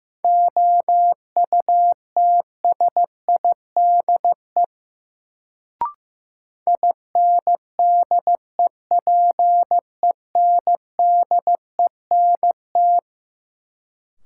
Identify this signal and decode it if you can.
Morse code, heard as a keyed tone, about 15 wpm, 700 Hz. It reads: OUTSIDE INDEPENDENT
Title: Sentences from Top 900 Words - New Word Drill - 15wpm